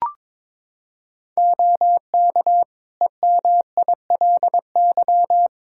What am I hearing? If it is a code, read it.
Morse code, heard as a keyed tone, about 22 wpm, 700 Hz. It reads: OK EMILY
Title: Practice QSO - Exchange 3 - More Info - 22wpm